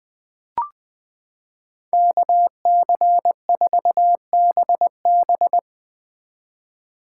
Morse code, heard as a keyed tone, about 20 wpm, 700 Hz. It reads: KC4BB
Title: 2023 POTA Activator Call Signs - 20wpm